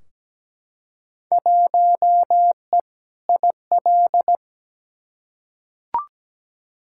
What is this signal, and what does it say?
Morse code, heard as a keyed tone, about 17 wpm, 700 Hz. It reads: 1E IL
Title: Field Day Exchanges - 17wpm